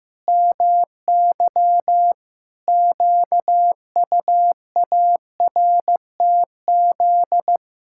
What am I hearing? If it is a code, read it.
Morse code, heard as a keyed tone, about 16 wpm, 700 Hz. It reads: MY QUARTZ
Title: Pangram Challenge v2 - 15wpm